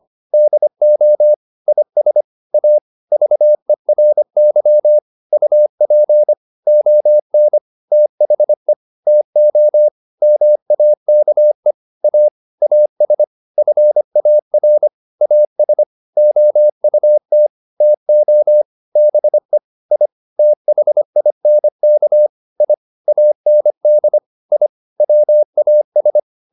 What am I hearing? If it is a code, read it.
Morse code, heard as a keyed tone, about 25 wpm, 600 Hz. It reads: DO IS A VERY UP ON THE TO MAKE A AS FAR AS OUT TO BE I THINK I AND I WAS